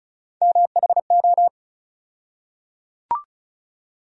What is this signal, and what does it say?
Morse code, heard as a keyed tone, about 35 wpm, 700 Hz. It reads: MHO
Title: USA Call Sign Postfixes - 3 Letters 35wpm